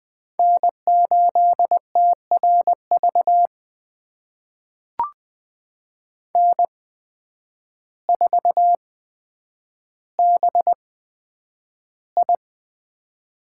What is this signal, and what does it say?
Morse code, heard as a keyed tone, about 20 wpm, spaced as for 9 wpm, 700 Hz. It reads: N8TRV N4BI
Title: USA Call Signs - 5 Characters - 8x Character Spacing - 20wpm